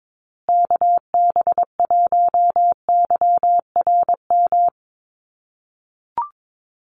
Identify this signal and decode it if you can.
Morse code, heard as a keyed tone, about 22 wpm, 700 Hz. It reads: KB1YRM
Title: USA Call Signs Repeated 2x 22wpm